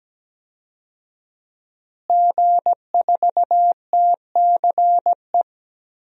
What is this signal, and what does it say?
Morse code, heard as a keyed tone, about 17 wpm, 700 Hz. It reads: G4TCE
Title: DXCC-100 Call Signs - 5 Characters - Repeated 3x - 17wpm